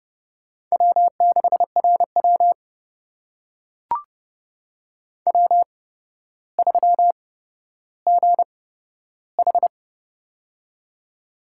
Morse code, heard as a keyed tone, about 31 wpm, spaced as for 13 wpm, 700 Hz. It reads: W6RW W3GH
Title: USA Call Signs - 4 Characters - 8x Character Spacing - 30wpm